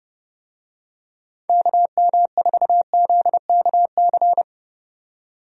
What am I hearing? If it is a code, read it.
Morse code, heard as a keyed tone, about 31 wpm, 700 Hz. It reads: KM4ZKC
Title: USA Call Signs - 6 Characters - 30wpm